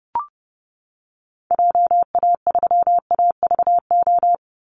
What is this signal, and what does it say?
Morse code, heard as a keyed tone, about 29 wpm, 700 Hz. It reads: JA3AVO
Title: Call Signs - 30wpm